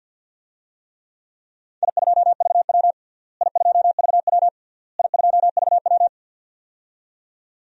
Morse code, heard as a keyed tone, about 52 wpm, 700 Hz. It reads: I2VW I2VW I2VW
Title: DXCC-100 Call Signs - 4 Characters - Repeated 3x - 50wpm